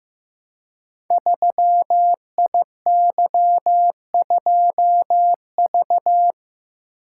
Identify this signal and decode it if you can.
Morse code, heard as a keyed tone, about 15 wpm, 700 Hz. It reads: 3IY2V
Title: Code Groups - 5 Letter:Numbers - 15wpm